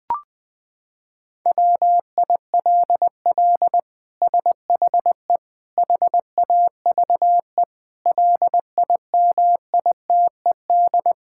Morse code, heard as a keyed tone, about 21 wpm, 700 Hz. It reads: WILL SHE HAVE LIMITED
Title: Sentences from Top 1000 Words - Review Part 6 - 20wpm